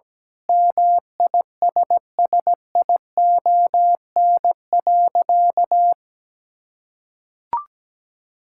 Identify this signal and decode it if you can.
Morse code, heard as a keyed tone, about 17 wpm, 700 Hz. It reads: MISSION.
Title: Sentences from Top 1600 Words 17wpm